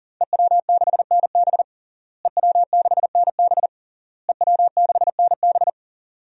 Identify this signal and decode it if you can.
Morse code, heard as a keyed tone, about 40 wpm, 700 Hz. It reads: EW6NB EW6NB EW6NB
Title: DXCC-100 Call Signs - 5 Characters - Repeated 3x - 40wpm